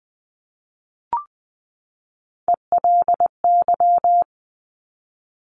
Morse code, heard as a keyed tone, about 20 wpm, 700 Hz. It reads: ELY
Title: Top Three Letter Combinations in Words 20wpm